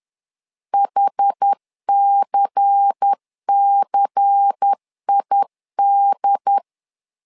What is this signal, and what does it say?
Morse code, heard as a keyed tone, about 10 wpm, 800 Hz. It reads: HCCID
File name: LEZIONI / 5 LBVC / 5 LBVC 10wpm 2349